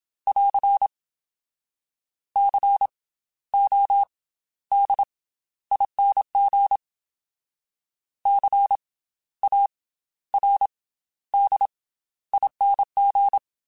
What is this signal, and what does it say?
Morse code, heard as a keyed tone, about 27 wpm, spaced as for 13 wpm, 800 Hz. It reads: <AR> CODING CARDING
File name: Special INGs / Special INGs-mp3s / Slow ing @25